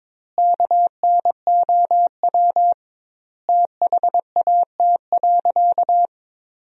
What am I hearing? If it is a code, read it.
Morse code, heard as a keyed tone, about 22 wpm, 700 Hz. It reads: KNOW THAT.
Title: 2x Word-Spacing - Sentences from Top 500 Words - 22wpm